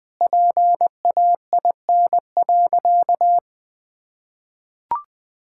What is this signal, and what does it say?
Morse code, heard as a keyed tone, about 19 wpm, 700 Hz. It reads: PAIN.
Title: Sentences from Top 1600 Words 20wpm